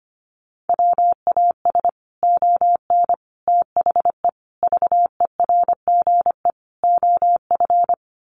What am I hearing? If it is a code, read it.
Morse code, heard as a keyed tone, about 25 wpm, 700 Hz. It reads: WAS ON THE VERGE OF
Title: Sets of 5 Words Encore 25wpm